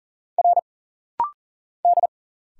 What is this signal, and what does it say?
Morse code, heard as a keyed tone, about 40 wpm, 700 Hz. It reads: R D
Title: Single Letter-Number Alt - Rapid-Fire 40wpm